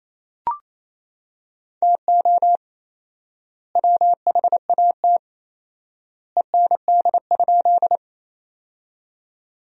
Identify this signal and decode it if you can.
Morse code, heard as a keyed tone, about 28 wpm, 700 Hz. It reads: TO WHAT END?
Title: 4x Word-Spacing - Sentences from Top 200 Words - 28wpm